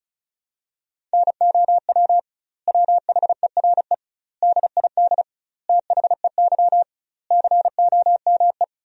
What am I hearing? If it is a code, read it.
Morse code, heard as a keyed tone, about 36 wpm, 700 Hz. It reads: NOW WHERE DID THEY COME
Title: 2x Word-Spacing - Sentences from Top 200 Words - 35wpm